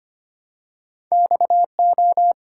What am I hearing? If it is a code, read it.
Morse code, heard as a keyed tone, about 25 wpm, 700 Hz. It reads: XO
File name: Two Letter-Number Combinations - 25wpm